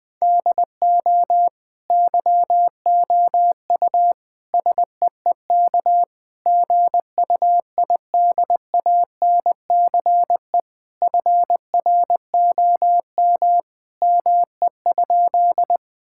Morse code, heard as a keyed tone, about 21 wpm, 700 Hz. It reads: DO YOU SEEK GUIDANCE FROM ME?
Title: Sentences from Top 1600 Words 20wpm